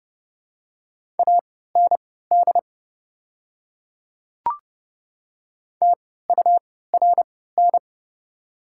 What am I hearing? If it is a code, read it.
Morse code, heard as a keyed tone, about 30 wpm, spaced as for 17 wpm, 700 Hz. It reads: AND TURN
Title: 3x Character Spacing - Top 300 Words - 30wpm